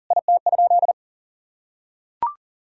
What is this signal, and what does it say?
Morse code, heard as a keyed tone, about 40 wpm, 700 Hz. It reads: IT?
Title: Sentences from Top 1000 Words - Review Part 4 - 40wpm